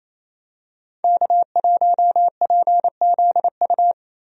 Morse code, heard as a keyed tone, about 27 wpm, 700 Hz. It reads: K1PZU
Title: USA Call Signs - 5 Characters - Repeated 2x - 28wpm